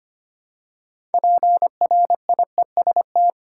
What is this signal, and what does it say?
Morse code, heard as a keyed tone, about 25 wpm, 700 Hz. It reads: PRIEST
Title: Six Letter Words - 25wpm